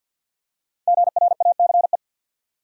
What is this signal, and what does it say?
Morse code, heard as a keyed tone, about 50 wpm, 700 Hz. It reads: GRACE
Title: USA Common Names - Max 5 Letters - 50wpm